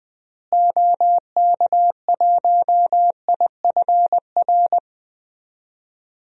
Morse code, heard as a keyed tone, about 21 wpm, 700 Hz. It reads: OK1IFR
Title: DXCC-100 Call Signs - 6 Characters - Repeated 3x - 20wpm